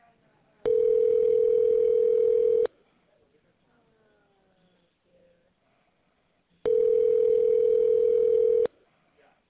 Ringback tone (North America), 440 + 480 Hz.